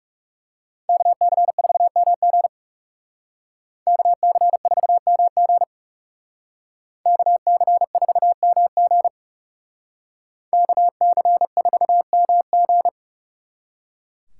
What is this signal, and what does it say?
Morse code, heard as a keyed tone, about 38 wpm, 700 Hz. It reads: KC4MG KC4MG KC4MG KC4MG